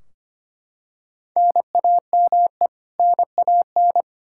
Morse code, heard as a keyed tone, about 25 wpm, 700 Hz. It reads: NAME NAN
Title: Practice QSO - Exchange 1 - Beginning - 25wpm